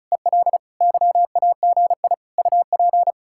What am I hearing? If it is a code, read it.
Morse code, heard as a keyed tone, about 46 wpm, 700 Hz. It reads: EL YAGI UP